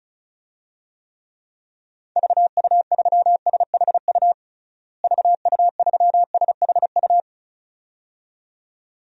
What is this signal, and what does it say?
Morse code, heard as a keyed tone, about 34 wpm, 700 Hz. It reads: VU3SHU VU3SHU
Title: DXCC-100 Call Signs - 6 Characters - Repeated 2x - 35wpm